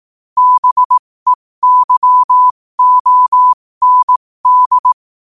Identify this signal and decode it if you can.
Morse code, heard as a keyed tone, about 18 wpm, spaced as for 15 wpm, 1000 Hz. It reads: BEYOND
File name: qotdfiles / 15wpm